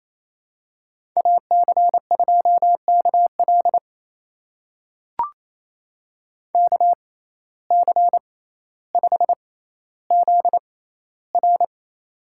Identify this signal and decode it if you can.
Morse code, heard as a keyed tone, about 28 wpm, spaced as for 14 wpm, 700 Hz. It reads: AC2KL KC5ZR
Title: USA Call Signs - 5 Characters - 6x Character Spacing - 28wpm